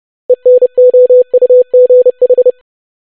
Morse code, heard as a keyed tone, about 30 wpm, 500 Hz. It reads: ENOUGH
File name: basic500-25wpm